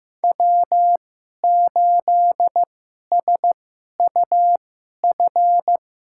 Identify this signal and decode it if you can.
Morse code, heard as a keyed tone, about 15 wpm, spaced as for 11 wpm, 700 Hz. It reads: W8SUF
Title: USA Call Signs - 5 Characters - 2x Character Spacing - 15wpm